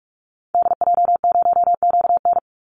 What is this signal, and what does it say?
Morse code, heard as a keyed tone, about 44 wpm, 700 Hz. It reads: DJ0QN
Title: Call Signs Repeated 2x - 45wpm